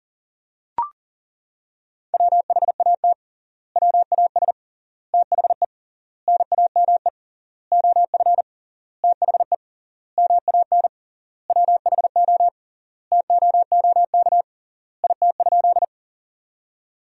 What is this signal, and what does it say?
Morse code, heard as a keyed tone, about 40 wpm, 700 Hz. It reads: WHAT WAS THE NAME OF THE MAN WHO TOOK IT?
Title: 3x Word-Spacing - Sentences from Top 300 Words - 40wpm